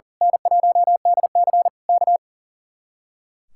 Morse code, heard as a keyed tone, about 38 wpm, 700 Hz. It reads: N1DC K
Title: Speed-Racing Common QSO Phrases Encore - 40wpm